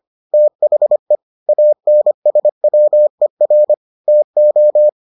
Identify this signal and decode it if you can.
Morse code, heard as a keyed tone, about 25 wpm, 600 Hz. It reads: THE ANSWER TO